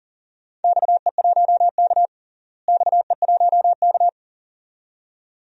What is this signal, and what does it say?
Morse code, heard as a keyed tone, about 40 wpm, 700 Hz. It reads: XE1K XE1K
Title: DXCC-100 Call Signs - 4 Characters - Repeated 2x - 40wpm